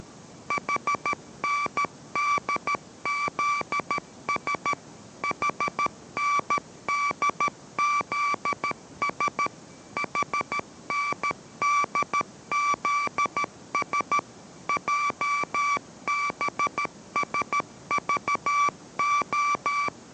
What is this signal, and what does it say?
Morse code, heard as a keyed tone, about 13 wpm, 1160 Hz. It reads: HNDZS HNDZS HNDZS JBSVO